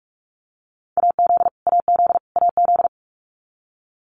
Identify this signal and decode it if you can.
Morse code, heard as a keyed tone, about 44 wpm, 700 Hz. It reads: AZ AZ AZ
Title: US States Repeated 3x 45wpm